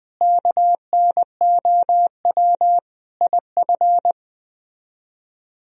Morse code, heard as a keyed tone, about 21 wpm, 700 Hz. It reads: KNOW IF